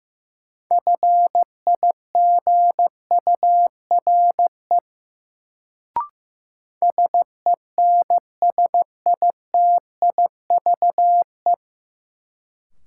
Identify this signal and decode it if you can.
Morse code, heard as a keyed tone, about 15 wpm, 700 Hz. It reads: FIGURE SENSITIVE
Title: Sentences from Top 1400 Words - New Word Drill Alt - 15wpm